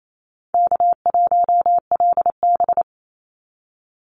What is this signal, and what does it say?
Morse code, heard as a keyed tone, about 28 wpm, 700 Hz. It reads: K1LB
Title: NA SOTA Chaser Callsigns 28wpm